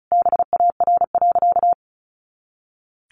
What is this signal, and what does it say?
Morse code, heard as a keyed tone, about 34 wpm, 700 Hz. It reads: BAR.
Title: Easy Phrases 35wpm